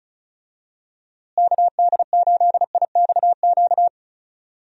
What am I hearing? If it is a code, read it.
Morse code, heard as a keyed tone, about 34 wpm, 700 Hz. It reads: KD8IXQ
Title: USA Call Signs - 6 Characters - Repeated 2x - 35wpm